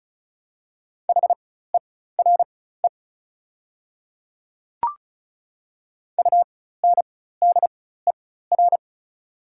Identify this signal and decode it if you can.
Morse code, heard as a keyed tone, about 36 wpm, spaced as for 16 wpm, 700 Hz. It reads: HERE UNDER
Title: 4x Character Spacing - Top 200 Words - 35wpm